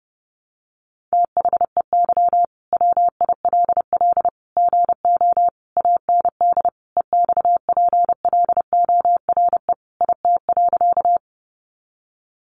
Morse code, heard as a keyed tone, about 30 wpm, 700 Hz. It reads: THEY WILL GO AND EXPLORE IT.
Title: Sentences from Top 1100 Words 30wpm